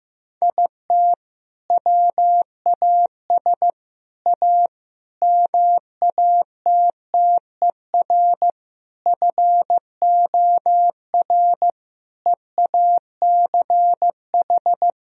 Morse code, heard as a keyed tone, about 15 wpm, 700 Hz. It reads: IT WAS A MATTER FOR EACH
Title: Sentences from Top 1000 Words - Review Part 4 - 15wpm